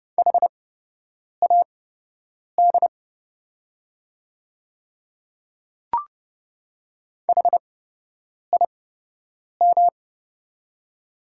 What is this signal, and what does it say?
Morse code, heard as a keyed tone, about 30 wpm, spaced as for 8 wpm, 700 Hz. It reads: HAD HIM
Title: 8x Character Spacing - Top 100 Words - 30wpm